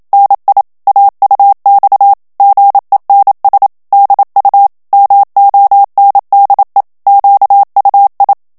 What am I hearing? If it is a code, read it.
Morse code, heard as a keyed tone, about 27 wpm, 800 Hz. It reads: NI AUX GENS DU MONDE QUI
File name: FRANCAIS TEXTE CLAIR 2 VITESSE 26 MOTS MINUTES